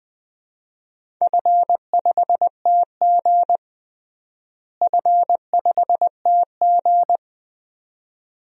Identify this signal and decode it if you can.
Morse code, heard as a keyed tone, about 20 wpm, 700 Hz. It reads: F5TG F5TG
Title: DXCC-100 Call Signs - 4 Characters - Repeated 2x - 20wpm